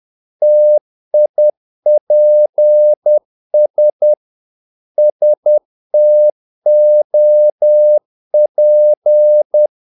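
Morse code, heard as a keyed tone, about 10 wpm, 600 Hz. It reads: TIPS STOP
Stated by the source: Morse practice of short Danish words at 10 wpm, from CW med Gnister.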